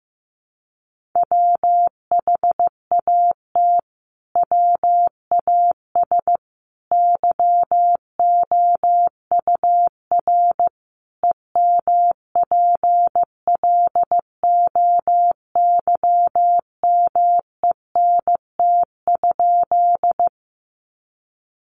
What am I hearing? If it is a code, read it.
Morse code, heard as a keyed tone, about 14 wpm, 700 Hz. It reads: WHAT WAS YOUR EMPLOYMENT?
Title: Sentences from Top 1500 Words 15wpm